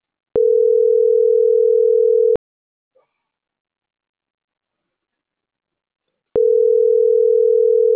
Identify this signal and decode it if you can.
Ringback tone (North America), 440 + 480 Hz.